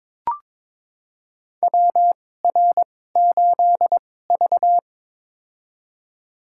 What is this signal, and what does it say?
Morse code, heard as a keyed tone, about 22 wpm, spaced as for 16 wpm, 700 Hz. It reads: WR8V
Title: USA Call Signs - 4 Characters - 2x Character Spacing - 22wpm